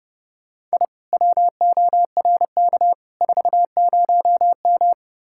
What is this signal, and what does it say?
Morse code, heard as a keyed tone, about 38 wpm, 700 Hz. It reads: I WORK 40M